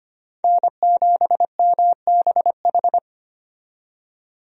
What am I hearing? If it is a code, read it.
Morse code, heard as a keyed tone, about 25 wpm, 700 Hz. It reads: N7MBH
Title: USA Call Signs - 5 Characters - 25wpm